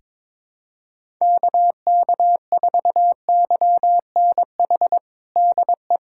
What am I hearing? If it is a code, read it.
Morse code, heard as a keyed tone, about 22 wpm, 700 Hz. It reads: KK4YNH DE
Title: USA Callsign DE Callsign - 6 Characters - 22wpm